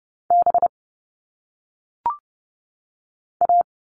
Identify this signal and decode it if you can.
Morse code, heard as a keyed tone, about 30 wpm, 700 Hz. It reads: B A
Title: All Cut Numbers - 1-Digit 30wpm